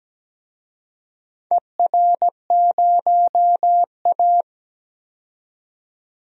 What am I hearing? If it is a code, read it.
Morse code, heard as a keyed tone, about 17 wpm, 700 Hz. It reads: ER0A
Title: DXCC-100 Call Signs - 4 Characters - 17wpm